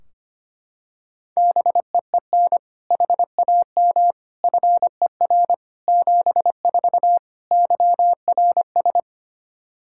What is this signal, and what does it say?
Morse code, heard as a keyed tone, about 25 wpm, 700 Hz. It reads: BEEN HAM FER 74 YRS
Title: Practice QSO - Exchange 3 - More Info - 25wpm